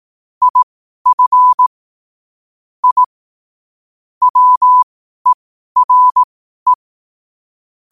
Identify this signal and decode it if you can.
Morse code, heard as a keyed tone, about 18 wpm, spaced as for 12 wpm, 1000 Hz. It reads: IF I WERE
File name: qotdfiles / 12wpm